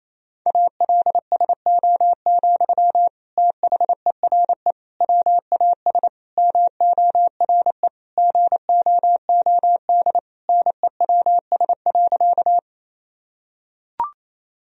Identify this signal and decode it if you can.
Morse code, heard as a keyed tone, about 28 wpm, 700 Hz. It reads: ALSO, THERE WAS MORE GOOD NEWS.
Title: Sentences from Top 1000 Words - Review Part 1 - 28wpm